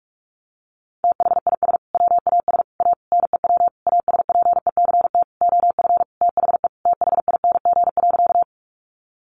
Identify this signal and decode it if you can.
Morse code, heard as a keyed tone, about 44 wpm, 700 Hz. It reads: THIS WAS A NEW ASPECT OF THE THING.
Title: Sentences from Top 1300 Words 45wpm